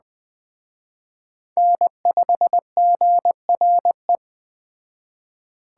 Morse code, heard as a keyed tone, about 20 wpm, 700 Hz. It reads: N5GRE
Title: Speed Racing - USA Call Signs 17wpm